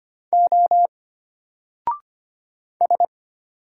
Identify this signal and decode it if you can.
Morse code, heard as a keyed tone, about 25 wpm, 700 Hz. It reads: O S
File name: Single-Letter-Number-25wpm